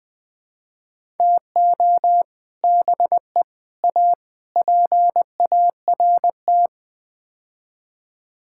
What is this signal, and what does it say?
Morse code, heard as a keyed tone, about 20 wpm, 700 Hz. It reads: TO BE A PART
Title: Build A Phrase - 5-Word Phrases - 20wpm